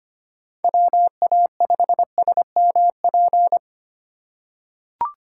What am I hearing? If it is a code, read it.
Morse code, heard as a keyed tone, about 25 wpm, 700 Hz. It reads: WA5SMP
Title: USA Call Signs - 6 Characters - 2x Character Spacing - 25wpm